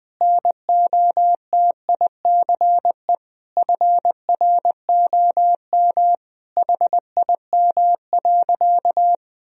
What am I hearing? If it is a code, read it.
Morse code, heard as a keyed tone, about 20 wpm, 700 Hz. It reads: NOTICE FROM HIM.
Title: Sentences from Top 1000 Words - Review Part 2 - 20wpm